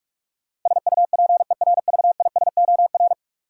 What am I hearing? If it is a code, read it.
Morse code, heard as a keyed tone, about 46 wpm, 700 Hz. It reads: SUPERVISOR
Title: Practice QSO - Exchange 3 - More Info - 45wpm